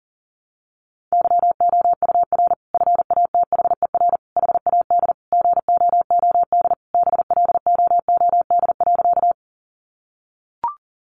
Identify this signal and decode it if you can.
Morse code, heard as a keyed tone, about 37 wpm, 700 Hz. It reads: YOUR FATHER HAD GOOD BLOOD.
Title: Sentences from Top 1100 Words 40wpm